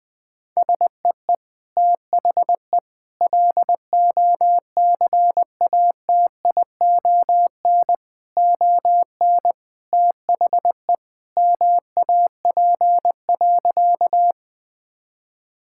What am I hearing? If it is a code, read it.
Morse code, heard as a keyed tone, about 20 wpm, 700 Hz. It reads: SEE THE LOCATION ON THE MAP.
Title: Sentences from Top 1000 Words - Review Part 6 - 20wpm